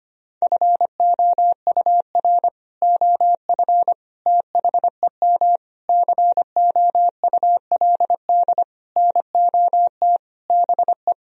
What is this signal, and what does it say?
Morse code, heard as a keyed tone, about 25 wpm, 700 Hz. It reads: FOUR OF THEM COULD NOT BE
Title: Sentences from Top 1000 Words - Review Part 1 - 25wpm